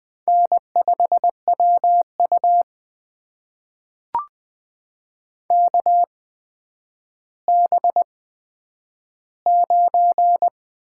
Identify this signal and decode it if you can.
Morse code, heard as a keyed tone, about 20 wpm, spaced as for 10 wpm, 700 Hz. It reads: N5WU KB9
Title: USA Call Signs - 8x Character Spacing - 20wpm